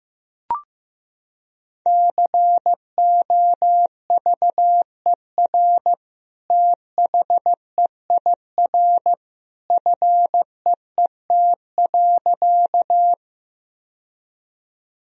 Morse code, heard as a keyed tone, about 15 wpm, 700 Hz. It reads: COVER THEIR FEET.
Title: Sentences from Top 1000 Words - Review Part 1 - 15wpm